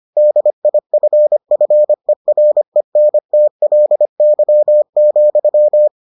Morse code, heard as a keyed tone, about 25 wpm, 600 Hz. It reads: DIFFERENTLY,